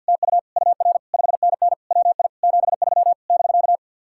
Morse code, heard as a keyed tone, about 64 wpm, 700 Hz. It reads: TU UR 5NN WI 73 <BK>